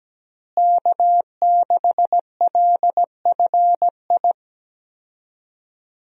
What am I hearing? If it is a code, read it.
Morse code, heard as a keyed tone, about 17 wpm, 700 Hz. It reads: K6LFI
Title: USA Call Signs - 5 Characters - Repeated 3x - 17wpm